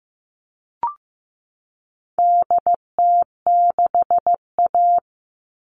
Morse code, heard as a keyed tone, about 15 wpm, 700 Hz. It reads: DT6A
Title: All Cut Numbers - 4-Digit 15wpm